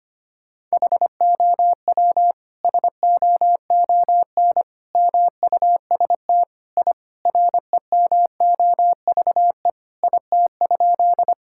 Morse code, heard as a keyed tone, about 25 wpm, 700 Hz. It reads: HOW SOON MUST I REMOVE IT?
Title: Sentences from Top 1000 Words - Review Part 5 - 25wpm